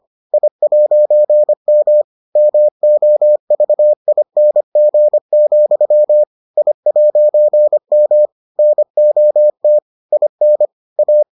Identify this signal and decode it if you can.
Morse code, heard as a keyed tone, about 25 wpm, 600 Hz. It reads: I'M MOVING, I'M NOT IN A